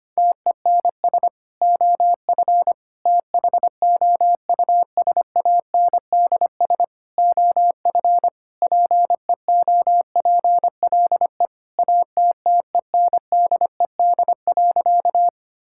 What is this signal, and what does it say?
Morse code, heard as a keyed tone, about 25 wpm, 700 Hz. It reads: TENS OF THOUSANDS OF PEOPLE ATTENDED.
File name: practice / 25wpm / Sentences with 4-Word Phrases - 25wpm